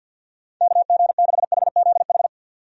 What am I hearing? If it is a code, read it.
Morse code, heard as a keyed tone, about 51 wpm, 700 Hz. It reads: KG6HZH
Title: USA Call Signs - 6 Characters - 50wpm